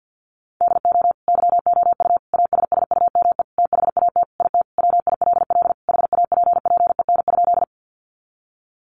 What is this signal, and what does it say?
Morse code, heard as a keyed tone, about 50 wpm, 700 Hz. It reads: DO YOU ASSUME THAT IT WILL HAPPEN?
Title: Sentences from Top 1500 Words 50wpm